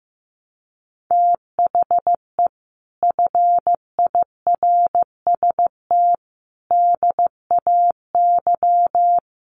Morse code, heard as a keyed tone, about 14 wpm, 700 Hz. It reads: THE FIRST DAY
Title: Sets of 5 Words Encore2 15wpm